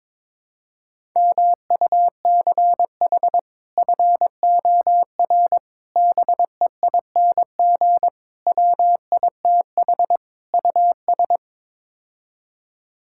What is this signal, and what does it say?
Morse code, heard as a keyed tone, about 22 wpm, 700 Hz. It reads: MUCH FOR BEING WITH US
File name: practice / 22wpm / Build A Phrase - 5-Word Phrases - 22wpm